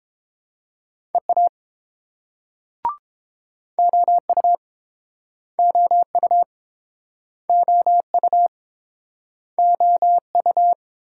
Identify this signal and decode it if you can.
Morse code, heard as a keyed tone, about 29 wpm, 690 Hz. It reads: EA OU OU OU OU